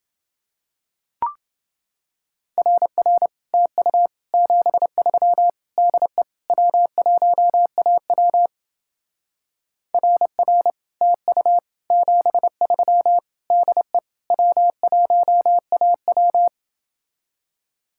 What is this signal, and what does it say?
Morse code, heard as a keyed tone, about 29 wpm, 700 Hz. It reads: RR TU 73 DE W1AW RR TU 73 DE W1AW